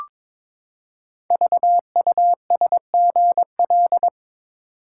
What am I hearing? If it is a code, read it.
Morse code, heard as a keyed tone, about 22 wpm, 700 Hz. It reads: VUSGL
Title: Code Groups - 5 Letters - 22wpm